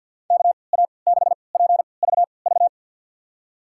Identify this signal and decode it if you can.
Morse code, heard as a keyed tone, about 53 wpm, spaced as for 29 wpm, 700 Hz. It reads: KA6PVV
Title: USA Call Signs - 3x Character Spacing - 50wpm